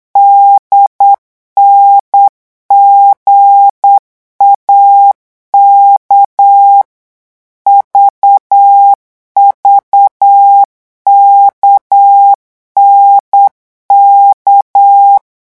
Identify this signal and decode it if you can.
Morse code, heard as a keyed tone, about 8 wpm, 800 Hz. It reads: DNGAK VVKNK
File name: morse mp3 / revision3-vitesse 8 mots